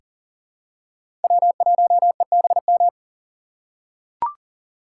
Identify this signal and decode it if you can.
Morse code, heard as a keyed tone, about 40 wpm, 700 Hz. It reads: W1EBM
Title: K1USN SST QSO Practice - 40wpm